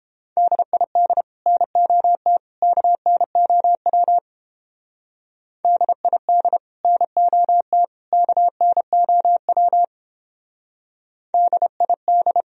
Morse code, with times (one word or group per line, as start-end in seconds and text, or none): 0.4-1.2 DID
1.5-2.4 NOT
2.6-4.2 KNOW
5.7-6.6 DID
6.8-7.8 NOT
8.1-9.8 KNOW
11.3-12.4 DID